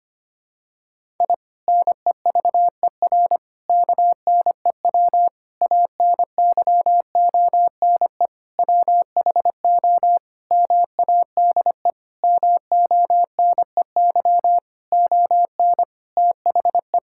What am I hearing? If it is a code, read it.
Morse code, heard as a keyed tone, about 25 wpm, 700 Hz. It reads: I NEVER KNEW ANYONE WHO MADE MONEY ON THE
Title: Sentences from Top 1000 Words - Review Part 6 - 25wpm